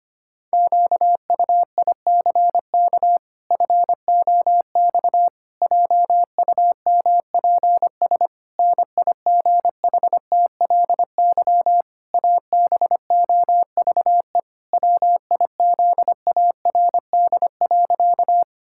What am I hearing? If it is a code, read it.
Morse code, heard as a keyed tone, about 25 wpm, 700 Hz. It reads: QUICK FOX JUMPS NIGHTLY ABOVE WIZARD.
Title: Pangram Challenge v2 - 25wpm